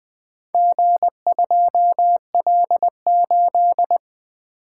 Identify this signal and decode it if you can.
Morse code, heard as a keyed tone, about 20 wpm, 700 Hz. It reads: G2L8
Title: Four Letter-Number Combinations - 20wpm